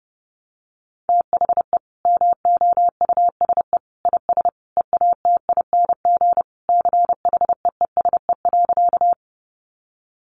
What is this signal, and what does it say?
Morse code, heard as a keyed tone, about 30 wpm, 700 Hz. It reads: THE MOUSE IS EATING CHEESE.
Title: Easy Phrases 30wpm